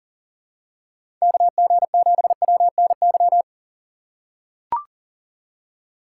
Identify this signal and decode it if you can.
Morse code, heard as a keyed tone, about 40 wpm, 700 Hz. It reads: KG7WNY
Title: USA Call Signs - 6x Character Spacing - 40wpm